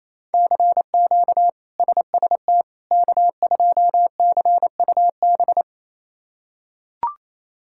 Morse code, heard as a keyed tone, about 28 wpm, 700 Hz. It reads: CQ SST K2CUB
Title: K1USN SST QSO Practice - 28wpm